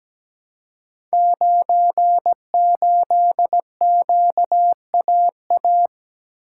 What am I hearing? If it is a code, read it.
Morse code, heard as a keyed tone, about 17 wpm, 700 Hz. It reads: 98QAA
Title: Code Groups - 5 Letter:Numbers - 17wpm